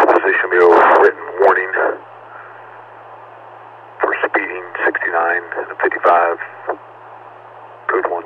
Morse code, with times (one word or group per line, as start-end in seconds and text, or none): none